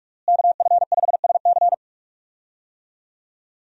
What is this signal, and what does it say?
Morse code, heard as a keyed tone, about 46 wpm, 700 Hz. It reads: KF5SC
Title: USA Call Signs - 5 Characters - Repeated 2x - 45wpm